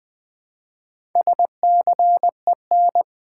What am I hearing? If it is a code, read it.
Morse code, heard as a keyed tone, about 20 wpm, 690 Hz. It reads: SCEN
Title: Build A Word - 2 Letters At A Time - 8-Letter Words - 20wpm